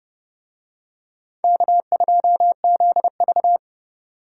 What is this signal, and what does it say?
Morse code, heard as a keyed tone, about 29 wpm, 700 Hz. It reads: K2ZV